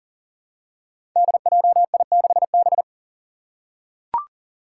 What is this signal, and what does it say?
Morse code, heard as a keyed tone, about 40 wpm, 700 Hz. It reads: DJI6B
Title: Five Challenging Letter-Number Combinations 40wpm